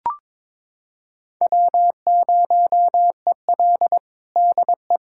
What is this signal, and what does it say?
Morse code, heard as a keyed tone, about 22 wpm, 700 Hz. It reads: W0EL DE
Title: USA Callsign DE Callsign - 4 Characters - 22wpm